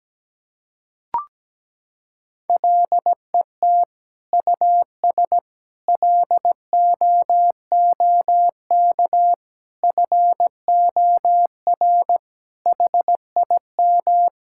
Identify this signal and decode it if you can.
Morse code, heard as a keyed tone, about 17 wpm, 700 Hz. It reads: LET US LOOK FOR HIM
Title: Sentences from Top 1600 Words 17wpm